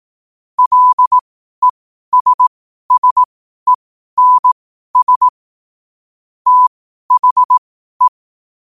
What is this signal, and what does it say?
Morse code, heard as a keyed tone, about 18 wpm, spaced as for 12 wpm, 1000 Hz. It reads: LESSENS THE
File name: qotdfiles / 12wpm